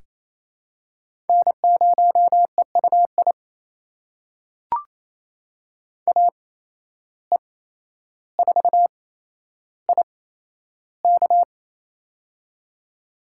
Morse code, heard as a keyed tone, about 28 wpm, spaced as for 12 wpm, 700 Hz. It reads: N0EUI AE4IK
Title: USA Call Signs - 5 Characters - 8x Character Spacing - 28wpm